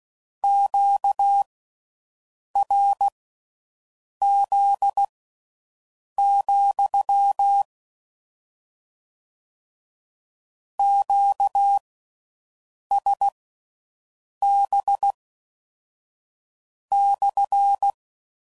Morse code, heard as a keyed tone, about 16 wpm, spaced as for 6 wpm, 800 Hz. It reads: QRZ, QSB /